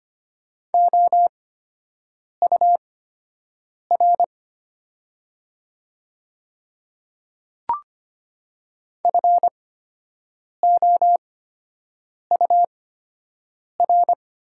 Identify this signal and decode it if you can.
Morse code, heard as a keyed tone, about 25 wpm, spaced as for 7 wpm, 700 Hz. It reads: OUR FOUR
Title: 8x Character Spacing - Top 300 Words - 25wpm